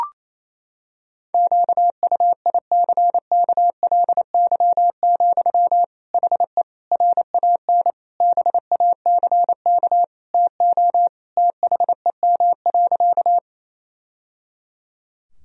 Morse code, heard as a keyed tone, about 28 wpm, 700 Hz. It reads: QUICKLY, HE RAN BACK TO THEM.
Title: Sentences from Top 1000 Words - Review Part 3 - 28wpm